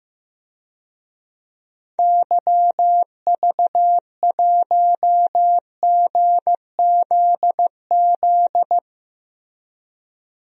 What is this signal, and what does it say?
Morse code, heard as a keyed tone, about 15 wpm, 700 Hz. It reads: YV1GZZ
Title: DXCC-100 Call Signs - 6 Characters - Repeated 2x - 15wpm